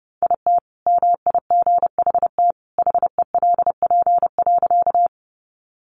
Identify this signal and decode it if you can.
Morse code, heard as a keyed tone, about 30 wpm, 700 Hz. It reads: IT MIGHT HELP.
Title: Sentences from Top 300 Words 30wpm